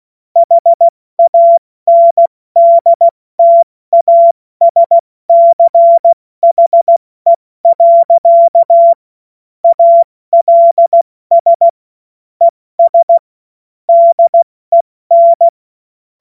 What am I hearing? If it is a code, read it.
Morse code, heard as a keyed tone, about 16 wpm, 680 Hz. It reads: HANDTASCHE. ALS ES DEN
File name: Morsekurs / Witze / Witz 08. November 2022 70 F80